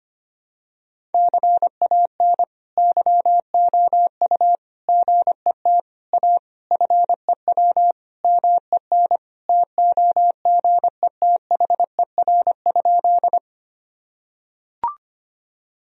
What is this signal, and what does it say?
Morse code, heard as a keyed tone, about 25 wpm, 700 Hz. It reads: CAN YOU GET A FEW MEN TOGETHER?
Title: Sentences from Top 1000 Words - Review Part 1 - 25wpm